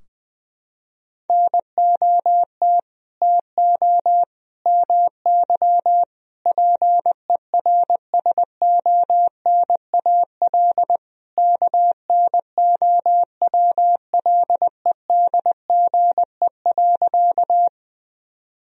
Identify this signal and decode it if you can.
Morse code, heard as a keyed tone, about 21 wpm, 700 Hz. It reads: NOT TO MY PERSONAL KNOWLEDGE.
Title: Sentences from Top 1000 Words - Review Part 4 - 20wpm